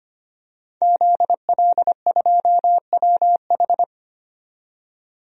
Morse code, heard as a keyed tone, about 25 wpm, 700 Hz. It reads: ZL2WH
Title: DXCC-100 Call Signs - 5 Characters - 25wpm